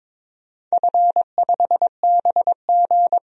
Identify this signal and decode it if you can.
Morse code, heard as a keyed tone, about 22 wpm, 700 Hz. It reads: F5BG
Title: DXCC-100 Call Signs - 4 Characters - 22wpm